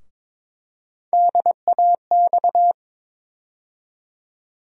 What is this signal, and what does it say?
Morse code, heard as a keyed tone, about 22 wpm, 700 Hz. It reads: DAX
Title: USA Call Sign Postfixes - 3 Letters 22wpm